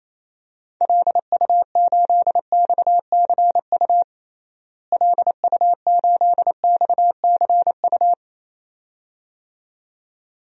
Morse code, heard as a keyed tone, about 28 wpm, 700 Hz. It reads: LU8XCU LU8XCU
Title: DXCC-100 Call Signs - 6 Characters - Repeated 2x - 28wpm